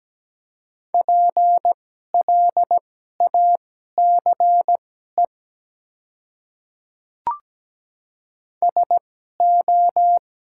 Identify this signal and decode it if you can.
Morse code, heard as a keyed tone, about 17 wpm, spaced as for 12 wpm, 700 Hz. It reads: PLACE SO
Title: 2x Character Spacing - Top 300 Words - 17wpm